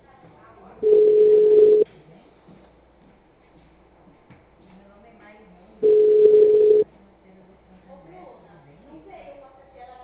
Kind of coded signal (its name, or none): telephone call progress tone